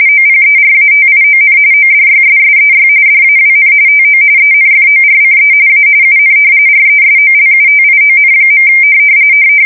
RTTY, about 45.45 baud. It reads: OPERATION ODYSSEY DAWN TO ENFORCE A UN RESOLUTION.
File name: fsk170 / usnavy5